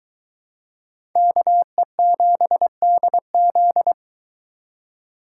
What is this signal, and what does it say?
Morse code, heard as a keyed tone, about 23 wpm, 700 Hz. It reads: KE7DZ